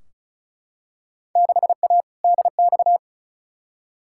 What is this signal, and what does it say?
Morse code, heard as a keyed tone, about 36 wpm, 700 Hz. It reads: 6A DX
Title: Field Day Exchanges Encore - 35wpm